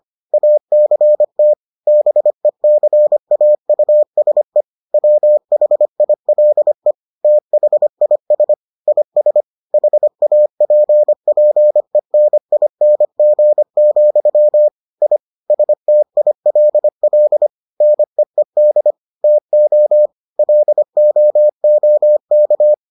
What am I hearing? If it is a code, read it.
Morse code, heard as a keyed tone, about 25 wpm, 600 Hz. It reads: ACT BECAUSE WHILE THIS IS HAPPENING, I STILL NEED TO LOOK